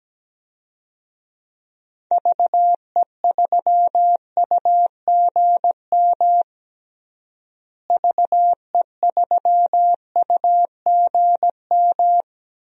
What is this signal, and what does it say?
Morse code, heard as a keyed tone, about 17 wpm, 700 Hz. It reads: VE3UGM VE3UGM
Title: DXCC-100 Call Signs - 6 Characters - Repeated 2x - 17wpm